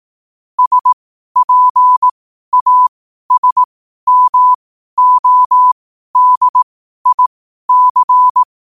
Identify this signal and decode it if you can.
Morse code, heard as a keyed tone, about 18 wpm, spaced as for 12 wpm, 1000 Hz. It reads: SPASMODIC
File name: qotdfiles / 12wpm